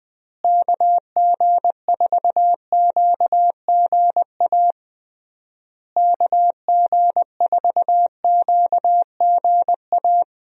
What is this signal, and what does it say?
Morse code, heard as a keyed tone, about 20 wpm, 700 Hz. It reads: KG4QGA KG4QGA
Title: USA Call Signs - 6 Characters - Repeated 3x - 20wpm